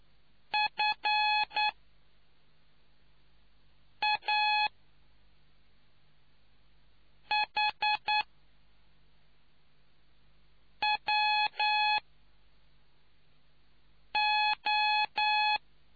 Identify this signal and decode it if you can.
Morse code, heard as a keyed tone, about 9 wpm, spaced as for 3 wpm, 850 Hz. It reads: FAHWO